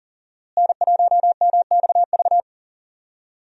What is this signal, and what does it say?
Morse code, heard as a keyed tone, about 40 wpm, 700 Hz. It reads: N1MXV
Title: USA Call Signs - 5 Characters - Repeated 3x - 40wpm